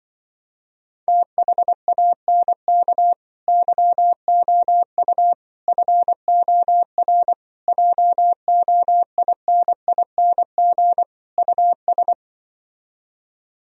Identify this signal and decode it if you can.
Morse code, heard as a keyed tone, about 24 wpm, 700 Hz. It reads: THANK YOU FOR JOINING US
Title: Speed-Racing - Sets of 5 Words 20wpm